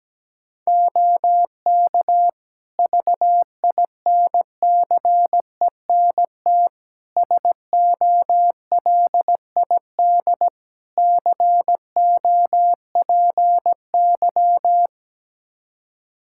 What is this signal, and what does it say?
Morse code, heard as a keyed tone, about 17 wpm, 700 Hz. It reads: OK VINCENT SOLID COPY
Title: Practice QSO - Exchange 3 - More Info - 17wpm